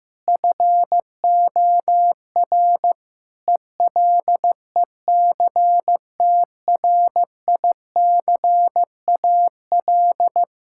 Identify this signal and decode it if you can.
Morse code, heard as a keyed tone, about 15 wpm, 700 Hz. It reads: FOR ELECTRICAL
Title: Sentences from Top 1000 Words - Review Part 6 - 15wpm